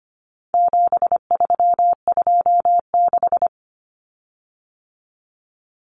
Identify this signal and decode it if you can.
Morse code, heard as a keyed tone, about 25 wpm, 700 Hz. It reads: 7326
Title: Common Cut Numbers - 4-Digit 25wpm